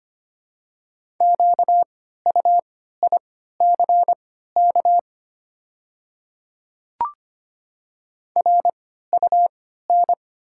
Morse code, heard as a keyed tone, about 25 wpm, spaced as for 14 wpm, 700 Hz. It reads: QUICK RUN
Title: 3x Character Spacing - Top 500 Words - 25wpm